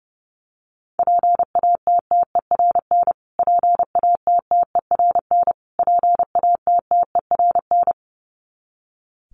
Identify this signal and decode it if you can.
Morse code, heard as a keyed tone, about 30 wpm, 700 Hz. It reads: PATTERN PATTERN PATTERN
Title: Top 500 Words Repeated 30wpm